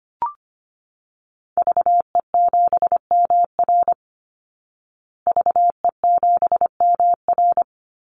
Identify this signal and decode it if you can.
Morse code, heard as a keyed tone, about 25 wpm, 700 Hz. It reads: VE7MR VE7MR
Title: Call Signs Repeated 2x - 25wpm